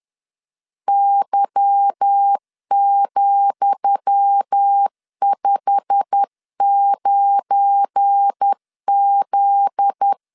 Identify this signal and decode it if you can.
Morse code, heard as a keyed tone, about 10 wpm, 800 Hz. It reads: Y,59Z
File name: LEZIONI / 10 SIGNS / 10D COMA 10wpm 3013